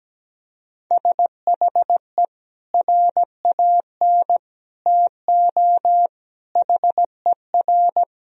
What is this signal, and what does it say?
Morse code, heard as a keyed tone, about 17 wpm, 700 Hz. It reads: SHE RAN TO HER
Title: Sentences from Top 1000 Words - Review Part 3 - 17wpm